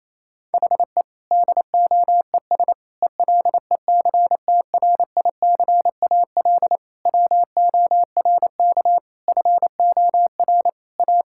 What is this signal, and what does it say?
Morse code, heard as a keyed tone, about 28 wpm, 700 Hz. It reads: HE DOES ELECTRICAL WORK FOR A
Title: Sentences from Top 1000 Words - Review Part 6 - 28wpm